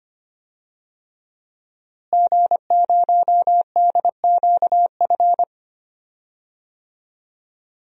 Morse code, heard as a keyed tone, about 25 wpm, 700 Hz. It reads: G0DQF
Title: DXCC-100 Call Signs - 5 Characters - Repeated 3x - 25wpm